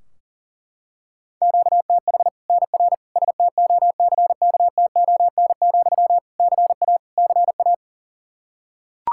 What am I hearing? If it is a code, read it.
Morse code, heard as a keyed tone, about 37 wpm, 700 Hz. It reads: QTH NR STOCKTON, CA CA
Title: Practice QSO - Exchange 1 - Beginning - 40wpm